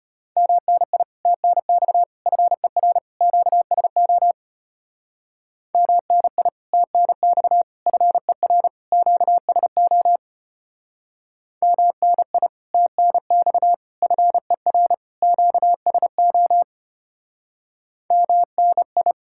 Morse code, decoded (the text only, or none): MNI TNX FER QSO MNI TNX FER QSO MNI TNX FER QSO MNI